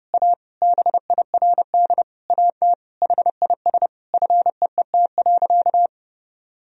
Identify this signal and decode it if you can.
Morse code, heard as a keyed tone, about 30 wpm, 700 Hz. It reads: A BIRD AT HIS FEET.
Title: Sentences from Top 1000 Words - Review Part 4 - 30wpm